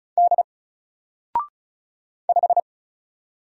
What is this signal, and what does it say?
Morse code, heard as a keyed tone, about 35 wpm, 700 Hz. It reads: D 5
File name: Single-Letter-Number-35wpm